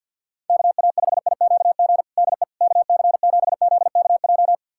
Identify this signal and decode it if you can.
Morse code, heard as a keyed tone, about 49 wpm, 700 Hz. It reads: KA5IQG DE KC7ZKJ
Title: USA Callsign DE Callsign - 6 Characters - 50wpm